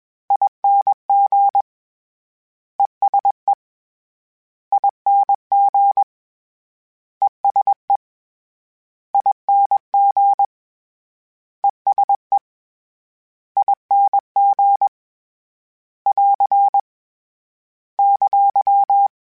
Morse code, heard as a keyed tone, about 21 wpm, 800 Hz. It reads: ING ESE ING ESE ING ESE ING <AR> !
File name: General / General-mp3s / ESE ING @20